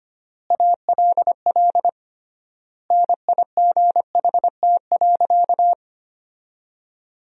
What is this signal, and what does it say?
Morse code, heard as a keyed tone, about 25 wpm, 700 Hz. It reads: ALL NIGHT.
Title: 3x Word-Spacing - Sentences from Top 500 Words - 25wpm